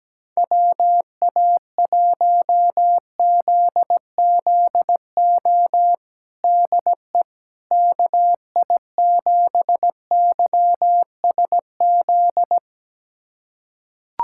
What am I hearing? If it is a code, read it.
Morse code, heard as a keyed tone, about 17 wpm, 700 Hz. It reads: WA1ZZO DE KI7YSZ
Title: USA Callsign DE Callsign - 6 Characters - 17wpm